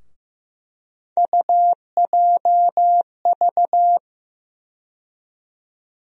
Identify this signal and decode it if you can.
Morse code, heard as a keyed tone, about 15 wpm, 700 Hz. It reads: UJV
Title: USA Call Sign Postfixes - 3 Letters 15wpm